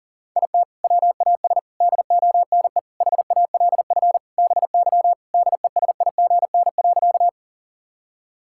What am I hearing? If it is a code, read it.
Morse code, heard as a keyed tone, about 40 wpm, 700 Hz. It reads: IT WAS DONE HALF BY DESIGN.
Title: Sentences from Top 1000 Words - Review Part 5 - 40wpm